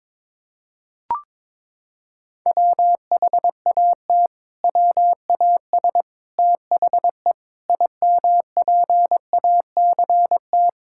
Morse code, heard as a keyed tone, about 22 wpm, 700 Hz. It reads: WHAT WAS THE IMPACT
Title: Sentences from Top 1000 Words - Review Part 6 - 22wpm